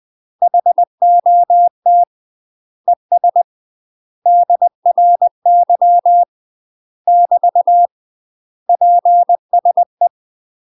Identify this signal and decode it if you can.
Morse code, heard as a keyed tone, about 20 wpm, 700 Hz. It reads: HOT ES DRY <BT> PSE
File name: UK MASTER QSOs TRAINING / 20WPM / UK MASTER QSOs 0015